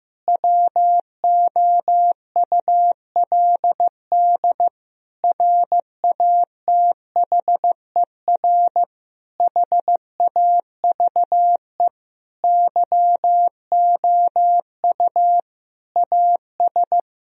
Morse code, heard as a keyed tone, about 16 wpm, 700 Hz. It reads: WOULD RATHER HAVE YOU AS
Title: Sentences from Top 1000 Words - Review Part 4 - 15wpm